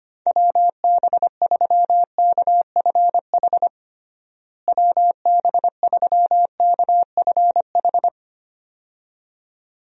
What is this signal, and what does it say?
Morse code, heard as a keyed tone, about 25 wpm, 700 Hz. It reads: WB3KFH WB3KFH
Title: USA Call Signs - 6 Characters - Repeated 2x - 25wpm